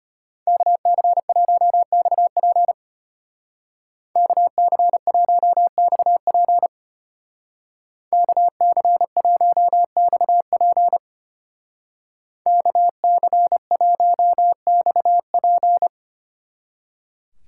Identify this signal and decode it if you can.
Morse code, heard as a keyed tone, about 32 wpm, 700 Hz. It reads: KC1XP KC1XP KC1XP KC1XP